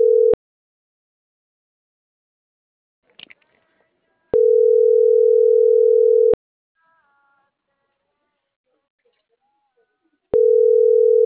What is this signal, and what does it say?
Ringback tone (North America), 440 + 480 Hz.